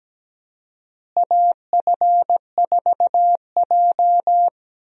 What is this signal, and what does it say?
Morse code, heard as a keyed tone, about 17 wpm, 700 Hz. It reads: AF4J
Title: USA Call Signs - 4 Characters - Repeated 2x - 17wpm